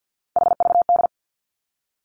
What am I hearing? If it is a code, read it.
Morse code, heard as a keyed tone, about 50 wpm, 700 Hz. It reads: HVD
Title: Three Challenging Letter-Number Combinations 50wpm